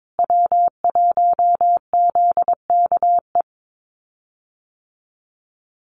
Morse code, heard as a keyed tone, about 22 wpm, 700 Hz. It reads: W1ZKE
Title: USA Call Signs Repeated 3x 22wpm